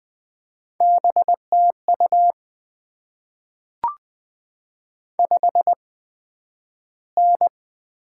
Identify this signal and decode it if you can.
Morse code, heard as a keyed tone, about 20 wpm, 700 Hz. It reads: B T U 5 N
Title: Top 20 QSO Elements - 8x Character Spacing - 20wpm